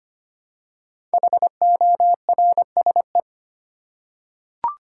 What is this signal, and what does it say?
Morse code, heard as a keyed tone, about 25 wpm, 700 Hz. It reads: HORSE
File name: 5 Letter Words - First 3 Letters Given - 25wpm